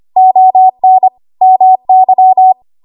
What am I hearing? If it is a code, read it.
Morse code, heard as a keyed tone, about 25 wpm, 750 Hz. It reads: ON MY
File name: Archive / 25 WPM / 25WPM